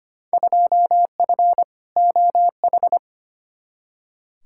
Morse code, heard as a keyed tone, about 25 wpm, 700 Hz. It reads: 2F OH
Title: Field Day Exchanges - 25wpm